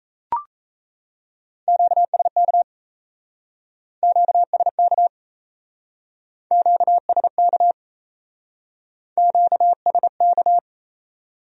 Morse code, heard as a keyed tone, about 37 wpm, 700 Hz. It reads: QSK QSK QSK QSK